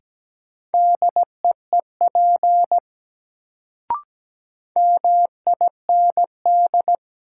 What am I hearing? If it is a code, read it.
Morse code, heard as a keyed tone, about 17 wpm, 700 Hz. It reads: DEEP MIND
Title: Sentences from Top 500 Words - New Word Drill - 17wpm